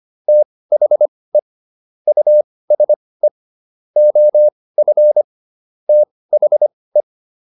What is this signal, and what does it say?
Morse code, heard as a keyed tone, about 25 wpm, spaced as for 18 wpm, 600 Hz. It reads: THE USE OF THE
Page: intermediate-practice-files